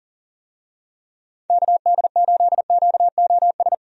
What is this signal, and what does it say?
Morse code, heard as a keyed tone, about 40 wpm, 700 Hz. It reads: KD8QOS
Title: USA Call Signs - 6 Characters - 40wpm